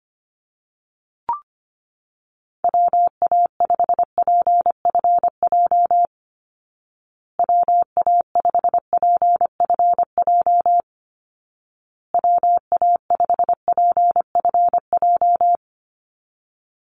Morse code, heard as a keyed tone, about 25 wpm, 700 Hz. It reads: WA5PFJ WA5PFJ WA5PFJ
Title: Call Signs Repeated 2x - 25wpm